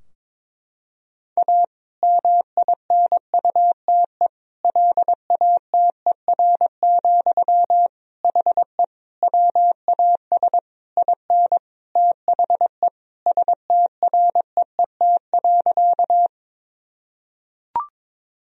Morse code, heard as a keyed tone, about 22 wpm, 700 Hz. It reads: A MINUTE LATER, HE WAS IN THE STREET.
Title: Sentences from Top 1000 Words - Review Part 4 - 22wpm